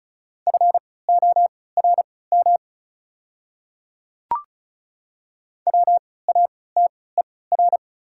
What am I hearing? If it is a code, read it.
Morse code, heard as a keyed tone, about 36 wpm, spaced as for 20 wpm, 690 Hz. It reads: FORM WATER
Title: 3x Character Spacing - Top 500 Words - 35wpm